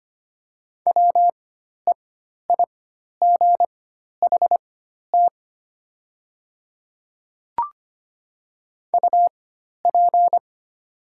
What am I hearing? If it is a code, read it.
Morse code, heard as a keyed tone, about 25 wpm, spaced as for 12 wpm, 700 Hz. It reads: WEIGHT UP
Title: 4x Character Spacing - Top 500 Words - 25wpm